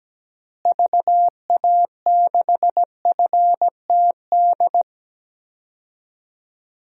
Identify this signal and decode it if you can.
Morse code, heard as a keyed tone, about 17 wpm, 700 Hz. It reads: VA6FTD
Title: DXCC-100 Call Signs - 6 Characters - Repeated 3x - 17wpm